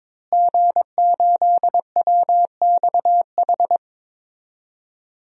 Morse code, heard as a keyed tone, about 22 wpm, 700 Hz. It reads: G8WXH
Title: DXCC-100 Call Signs - 5 Characters - Repeated 2x - 22wpm